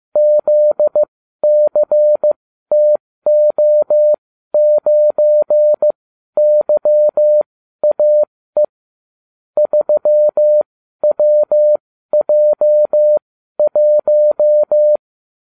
Morse code, heard as a keyed tone, about 15 wpm, spaced as for 12 wpm, 600 Hz. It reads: ZCTO9YAE 3WJ1